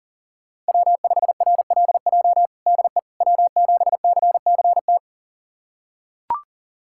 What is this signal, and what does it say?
Morse code, heard as a keyed tone, about 40 wpm, 700 Hz. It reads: W5RLJ DE W7CCT
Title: USA Callsign DE Callsign - 5 Characters - 40wpm